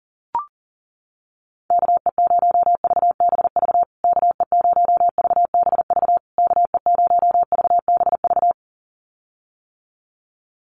Morse code, heard as a keyed tone, about 40 wpm, 700 Hz. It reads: KE0VBV KE0VBV KE0VBV
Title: USA Call Signs Repeated 3x 40wpm